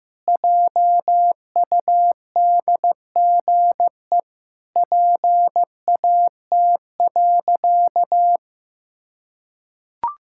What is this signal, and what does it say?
Morse code, heard as a keyed tone, about 15 wpm, 700 Hz. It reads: JUDGE PAT.
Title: Pangram Challenge v2 - 15wpm